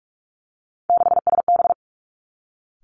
Morse code, heard as a keyed tone, about 46 wpm, 700 Hz. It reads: 6SB
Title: Three Challenging Letter-Number Combinations 45wpm